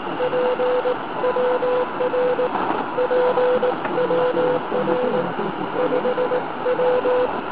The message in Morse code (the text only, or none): PWR PWR 5W